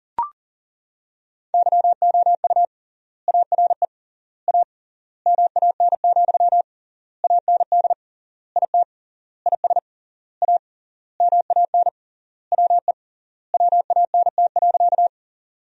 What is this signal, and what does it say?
Morse code, heard as a keyed tone, about 40 wpm, 700 Hz. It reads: YOU ARE A MAN, AND IT IS A MAN WE WANT.
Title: 3x Word-Spacing - Sentences from Top 200 Words - 40wpm